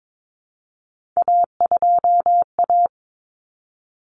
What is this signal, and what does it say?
Morse code, heard as a keyed tone, about 22 wpm, 700 Hz. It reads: A2A
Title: Common Cut Numbers - 3-Digit 22wpm